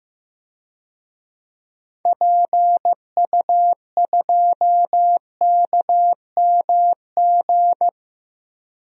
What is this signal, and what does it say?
Morse code, heard as a keyed tone, about 15 wpm, 700 Hz. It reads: PU2KMG
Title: DXCC-100 Call Signs - 6 Characters - 15wpm